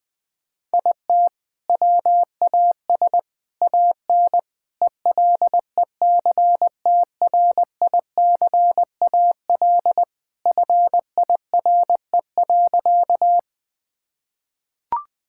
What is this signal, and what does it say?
Morse code, heard as a keyed tone, about 20 wpm, 700 Hz. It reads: IT WAS AN ELECTRICAL FIRE.
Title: Sentences from Top 1000 Words - Review Part 6 - 20wpm